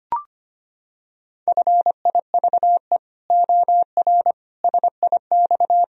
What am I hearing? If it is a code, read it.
Morse code, heard as a keyed tone, about 25 wpm, 700 Hz. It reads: FIVE OR SIX
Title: Sentences from Top 1000 Words - Review Part 3 - 25wpm